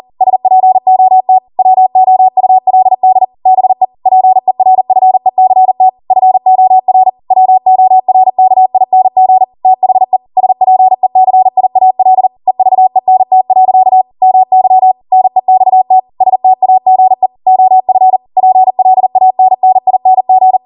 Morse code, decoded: SPOT WOULD BE PERFECT FOR WORKING THE SPECIAL EVENT. MY NEXT STAGE OF PLANNING

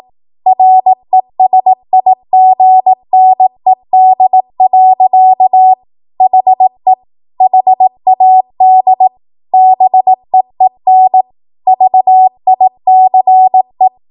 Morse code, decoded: RESIGNED. HE HAD BEEN VICE

